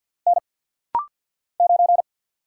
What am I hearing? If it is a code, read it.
Morse code, heard as a keyed tone, about 48 wpm, 700 Hz. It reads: N 9
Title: Single Letter-Number - Rapid-Fire 50wpm